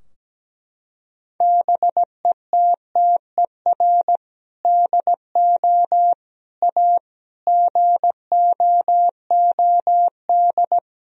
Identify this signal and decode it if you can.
Morse code, heard as a keyed tone, about 17 wpm, 700 Hz. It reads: BETTER DO A GOOD
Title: Sentences from Top 1000 Words - Review Part 3 - 17wpm